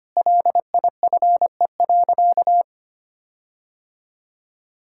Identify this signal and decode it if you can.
Morse code, heard as a keyed tone, about 25 wpm, 700 Hz. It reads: LIFE.
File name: practice / 25wpm / Difficult Sentences with Hints - 25wpm